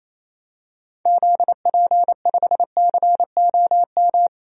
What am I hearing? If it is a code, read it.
Morse code, heard as a keyed tone, about 28 wpm, 700 Hz. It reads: ZP5COM
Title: DXCC-100 Call Signs - 6 Characters - 28wpm